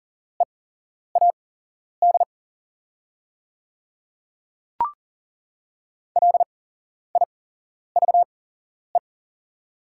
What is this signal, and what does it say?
Morse code, heard as a keyed tone, about 37 wpm, spaced as for 11 wpm, 700 Hz. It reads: EAD LIVE
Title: 8x Character Spacing - Top 300 Words - 40wpm